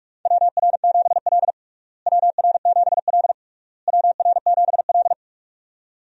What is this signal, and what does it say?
Morse code, heard as a keyed tone, about 46 wpm, 700 Hz. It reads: WR7L WR7L WR7L
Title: USA Call Signs - 4 Characters - Repeated 3x - 45wpm